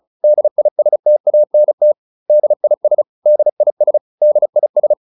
Morse code, heard as a keyed tone, about 34 wpm, 600 Hz. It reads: DISTANT DIS DIS DIS